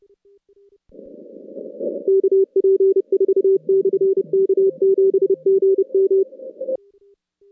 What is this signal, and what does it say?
Morse code, heard as a keyed tone, about 29 wpm, 390 Hz. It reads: KP4/K7GM
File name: contests / CW-SS-k5tr- -2200